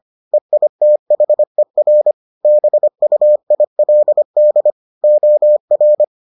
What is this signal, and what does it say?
Morse code, heard as a keyed tone, about 25 wpm, 600 Hz. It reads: EITHER BUILD OR